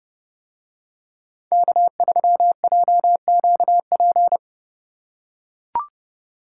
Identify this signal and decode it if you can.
Morse code, heard as a keyed tone, about 29 wpm, 700 Hz. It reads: K3JQP